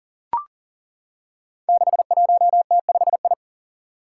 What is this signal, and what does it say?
Morse code, heard as a keyed tone, about 40 wpm, 700 Hz. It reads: 61T5I
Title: Five Challenging Letter-Number Combinations 40wpm